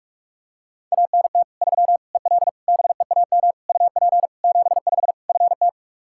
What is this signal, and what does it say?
Morse code, heard as a keyed tone, about 41 wpm, 700 Hz. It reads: ANT 3 EL BEAM UP 75 FT
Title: Practice QSO - Exchange 2 - Info - 45wpm